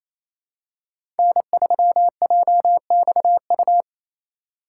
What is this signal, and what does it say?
Morse code, heard as a keyed tone, about 27 wpm, 700 Hz. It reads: N3JXU
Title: USA Call Signs - 8x Character Spacing - 28wpm